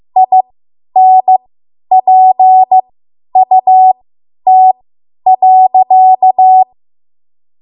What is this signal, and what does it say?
Morse code, heard as a keyed tone, about 15 wpm, spaced as for 10 wpm, 750 Hz. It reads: INPUT.
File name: Archive / 10 WPM / 10WPM